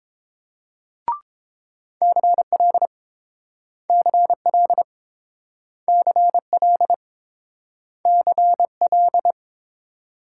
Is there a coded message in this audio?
Yes: Morse code from 2.0 to 9.3 s, heard as a keyed tone, about 30 wpm, 700 Hz. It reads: CL CL CL CL